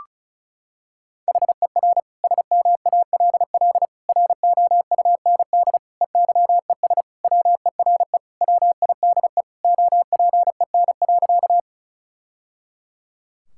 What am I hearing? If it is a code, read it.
Morse code, heard as a keyed tone, about 36 wpm, 700 Hz. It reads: HER SMALL ROUND EYES WERE WIDE OPEN.
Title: Sentences from Top 1000 Words - Review Part 6 - 35wpm